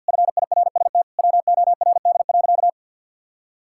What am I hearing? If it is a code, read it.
Morse code, heard as a keyed tone, about 50 wpm, 700 Hz. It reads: FIRST WORD.
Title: Pangram Challenge v2 - 50wpm